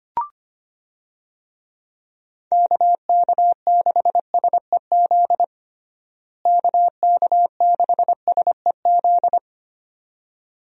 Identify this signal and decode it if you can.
Morse code, heard as a keyed tone, about 25 wpm, 700 Hz. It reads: KK6SEZ KK6SEZ
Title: USA Call Signs - 6 Characters - Repeated 2x - 25wpm